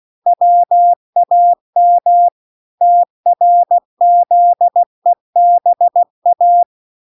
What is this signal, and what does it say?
Morse code, heard as a keyed tone, about 16 wpm, 700 Hz. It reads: WAM TRZEBA